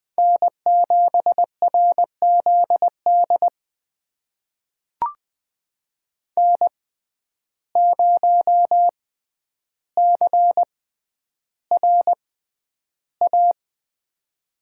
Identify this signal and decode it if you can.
Morse code, heard as a keyed tone, about 20 wpm, spaced as for 10 wpm, 700 Hz. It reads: N7RZD N0CRA
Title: USA Call Signs - 5 Characters - 6x Character Spacing - 20wpm